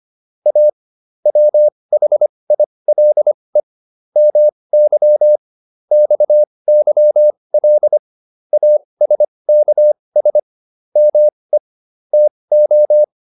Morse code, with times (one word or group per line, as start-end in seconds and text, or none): 0.5-0.7 A
1.3-3.6 WHILE
4.2-5.4 MY
5.9-8.0 XYL
8.5-10.4 ASKS
11.0-11.6 ME
12.1-13.0 TO